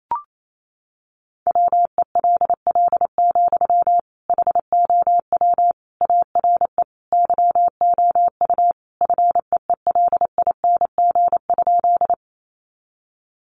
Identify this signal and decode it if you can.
Morse code, heard as a keyed tone, about 28 wpm, 700 Hz. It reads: WELL, HOW ARE YOU FEELING?
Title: Sentences from Top 1200 Words 28wpm